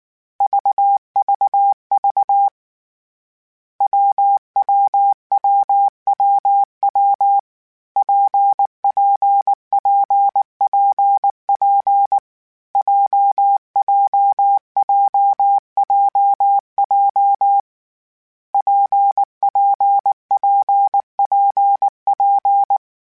Morse code, heard as a keyed tone, about 19 wpm, 800 Hz. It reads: VVV WWWWW PPPPP JJJJJ PPPPP
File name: Variable Speed / Variable Speed-mp3s / WPJ WPJ1 sp @18